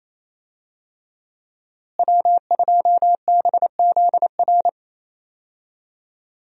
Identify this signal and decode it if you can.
Morse code, heard as a keyed tone, about 28 wpm, 700 Hz. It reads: W2BZR
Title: USA Call Signs - 5 Characters - Repeated 3x - 28wpm